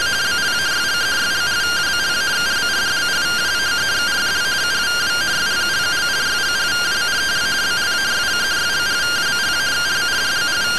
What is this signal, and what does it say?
Two-tone teleprinter signal, about 170 Hz shift.